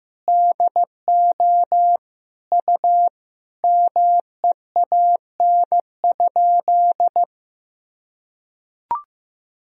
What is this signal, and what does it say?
Morse code, heard as a keyed tone, about 15 wpm, 700 Hz. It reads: DO U MEAN?
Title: Sentences from Top 200 Words and Abbreviations 15wpm